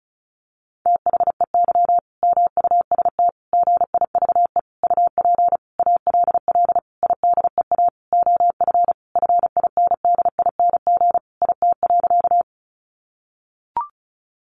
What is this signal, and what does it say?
Morse code, heard as a keyed tone, about 34 wpm, 700 Hz. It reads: THEY MUST GIVE UP ALL IDEA OF FINDING IT.
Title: Sentences from Top 1100 Words 35wpm